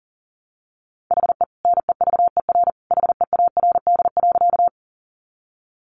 Morse code, heard as a keyed tone, about 40 wpm, 700 Hz. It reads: HE NEVER HEARD.
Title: Sentences from Top 400 Words 40wpm